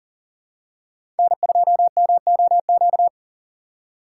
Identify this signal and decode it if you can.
Morse code, heard as a keyed tone, about 40 wpm, 700 Hz. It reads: N2MOQ
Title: USA Call Signs - 5 Characters - 2x Character Spacing - 40wpm